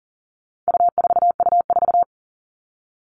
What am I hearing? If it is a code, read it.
Morse code, heard as a keyed tone, about 40 wpm, 700 Hz. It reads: U4U4
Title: All Cut Numbers - 4-Digit 40wpm